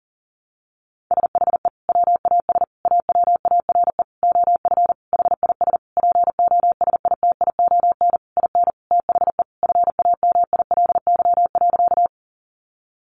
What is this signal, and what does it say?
Morse code, heard as a keyed tone, about 37 wpm, 700 Hz. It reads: SHE WAS AWARE OF HIS POSITION IN THE FAMILY.
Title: Sentences from Top 1200 Words 40wpm